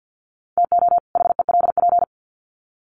Morse code, heard as a keyed tone, about 50 wpm, 700 Hz. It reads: TO HELP
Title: Sets of 2 Words 50wpm